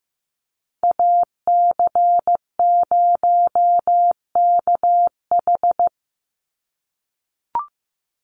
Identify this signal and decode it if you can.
Morse code, heard as a keyed tone, about 15 wpm, 700 Hz. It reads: AC0KH
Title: USA Call Signs 15wpm